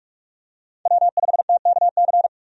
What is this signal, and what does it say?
Morse code, heard as a keyed tone, about 41 wpm, 700 Hz. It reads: W5TKC
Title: USA Call Signs - 5 Characters - 45wpm